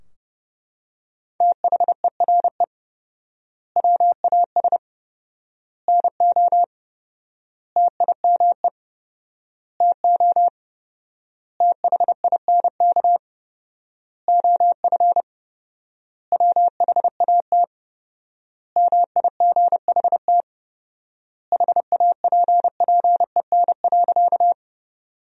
Morse code, heard as a keyed tone, about 30 wpm, 700 Hz. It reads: THERE WAS NO TIME TO THINK OF WHAT MIGHT HAPPEN.
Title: 4x Word-Spacing - Sentences from Top 400 Words - 30wpm